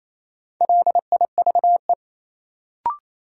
Morse code, heard as a keyed tone, about 28 wpm, 700 Hz. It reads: LIVE
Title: Sentences from Top 200 Words - New Word Drill - 28wpm